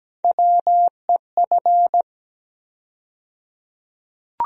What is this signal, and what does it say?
Morse code, heard as a keyed tone, about 17 wpm, 700 Hz. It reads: WEF
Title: USA Call Sign Postfixes - 3 Letters 17wpm